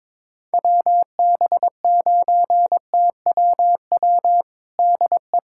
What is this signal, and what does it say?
Morse code, heard as a keyed tone, about 22 wpm, 700 Hz. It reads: WB9TWW DE
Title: USA Callsign DE Callsign - 6 Characters - 22wpm